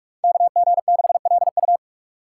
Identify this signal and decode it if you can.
Morse code, heard as a keyed tone, about 46 wpm, 700 Hz. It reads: KG6LU
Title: USA Call Signs - 5 Characters - 8x Character Spacing - 45wpm